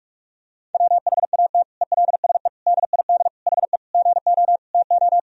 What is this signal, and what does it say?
Morse code, heard as a keyed tone, about 41 wpm, 700 Hz. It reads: WHAT ELSE DID HE GO TO
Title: Sentences from Top 1000 Words - Review Part 1 - 45wpm